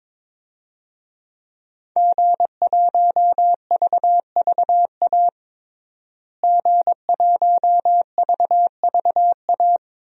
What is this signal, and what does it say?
Morse code, heard as a keyed tone, about 22 wpm, 700 Hz. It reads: G1VVA G1VVA
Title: DXCC-100 Call Signs - 5 Characters - Repeated 2x - 22wpm